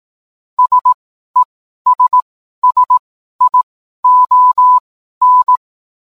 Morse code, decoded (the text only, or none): SESSION